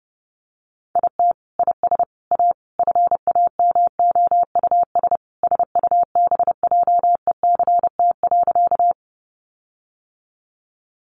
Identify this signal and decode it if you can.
Morse code, heard as a keyed tone, about 30 wpm, 700 Hz. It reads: IT IS A FAMOUS SUBJECT.
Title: Sentences from Top 1500 Words 30wpm